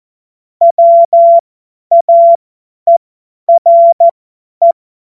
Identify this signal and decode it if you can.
Morse code, heard as a keyed tone, about 14 wpm, spaced as for 10 wpm, 680 Hz. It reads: WAERE
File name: Morsekurs / Witze / Witz 23 Januar 2024 50 F70